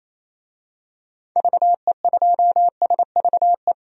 Morse code, heard as a keyed tone, about 27 wpm, 700 Hz. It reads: VE2SVE
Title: DXCC-100 Call Signs - 6 Characters - 28wpm